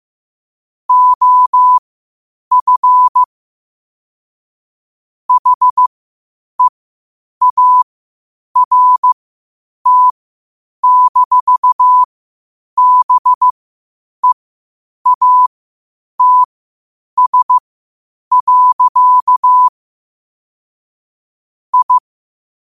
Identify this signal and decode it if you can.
Morse code, heard as a keyed tone, about 15 wpm, spaced as for 8 wpm, 1000 Hz. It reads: OF HEART-BEATS. I
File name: qotdfiles / 8wpm